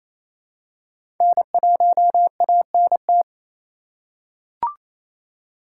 Morse code, heard as a keyed tone, about 28 wpm, 700 Hz. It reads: N1ANT
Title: USA Call Signs - 5 Characters - 2x Character Spacing - 28wpm